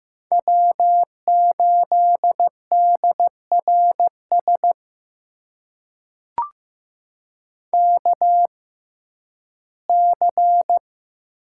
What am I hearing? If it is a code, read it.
Morse code, heard as a keyed tone, about 15 wpm, spaced as for 11 wpm, 700 Hz. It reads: W8DRS KC
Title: USA Call Signs - 5 Characters - 6x Character Spacing - 15wpm